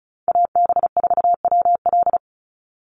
Morse code, heard as a keyed tone, about 34 wpm, 700 Hz. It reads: AB4WL
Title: NA SOTA Chaser Callsigns 35wpm